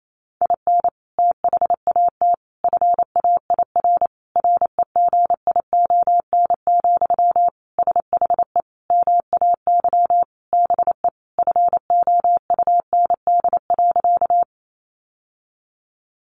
Morse code, heard as a keyed tone, about 28 wpm, 700 Hz. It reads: IN THAT FAIR REGION, SHE MAY BE FOUND.
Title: Sentences from Top 1500 Words 28wpm